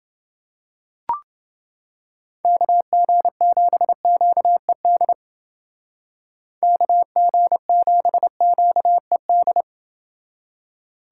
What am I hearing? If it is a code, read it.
Morse code, heard as a keyed tone, about 28 wpm, 700 Hz. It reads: KG7QED KG7QED